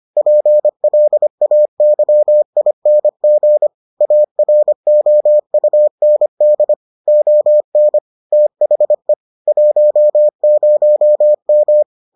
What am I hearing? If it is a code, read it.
Morse code, heard as a keyed tone, about 25 wpm, 600 Hz. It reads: PLAYING AROUND ON THE 10M